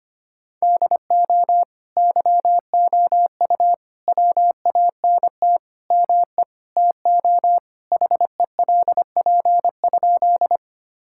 Morse code, heard as a keyed tone, about 25 wpm, 700 Hz. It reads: DO YOU WANT ME TO HELP?
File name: practice / 25wpm / Sentences with 4-Word Phrases - 25wpm